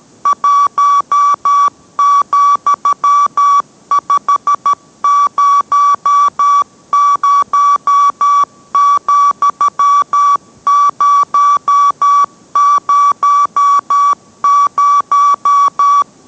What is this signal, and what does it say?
Morse code, heard as a keyed tone, about 13 wpm, 1150 Hz. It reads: 1,500,000